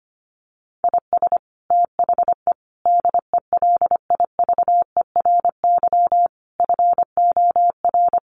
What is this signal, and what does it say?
Morse code, heard as a keyed tone, about 25 wpm, 700 Hz. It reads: IS THE DELIVERY FOR
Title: Sentences from Top 1500 Words 25wpm